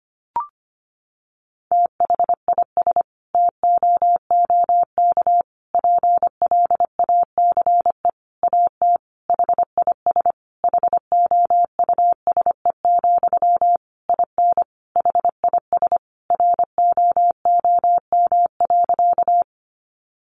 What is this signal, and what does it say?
Morse code, heard as a keyed tone, about 25 wpm, 700 Hz. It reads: THIS TOOK PLACE AT HIS HOUSE, IN HIS ROOM.
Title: Sentences from Top 300 Words 25wpm